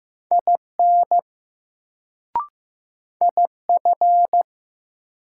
Morse code, heard as a keyed tone, about 15 wpm, 700 Hz. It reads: IN IF
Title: Sentences from Top 100 Words - New Word Drill - 15wpm